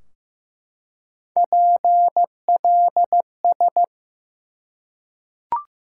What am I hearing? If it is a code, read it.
Morse code, heard as a keyed tone, about 15 wpm, 700 Hz. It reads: PLS
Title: Top 180 QSO Elements 15wpm